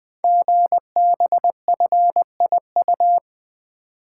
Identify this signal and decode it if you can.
Morse code, heard as a keyed tone, about 20 wpm, 700 Hz. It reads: GBFIU
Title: Code Groups - 5 Letters - 20wpm